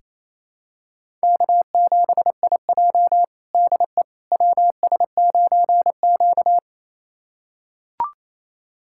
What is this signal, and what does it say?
Morse code, heard as a keyed tone, about 28 wpm, 700 Hz. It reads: K7IJ DE WS9Q
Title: USA Callsign DE Callsign - 4 Characters - 28wpm